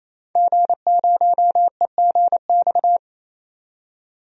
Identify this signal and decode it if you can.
Morse code, heard as a keyed tone, about 28 wpm, 700 Hz. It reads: G0EGX
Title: DXCC-100 Call Signs - 5 Characters - 28wpm